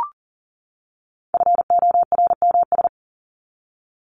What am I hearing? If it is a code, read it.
Morse code, heard as a keyed tone, about 40 wpm, 700 Hz. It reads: FORMS
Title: ES:S Suffix - Top 100 Words with Base Words 40wpm